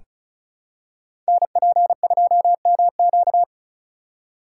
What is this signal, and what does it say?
Morse code, heard as a keyed tone, about 35 wpm, 690 Hz. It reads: NP2MQ
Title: Code Groups - 5 Letter:Numbers - 35wpm